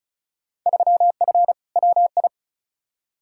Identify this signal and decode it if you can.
Morse code, heard as a keyed tone, about 36 wpm, 700 Hz. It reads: 3F WI
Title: Field Day Exchanges - 35wpm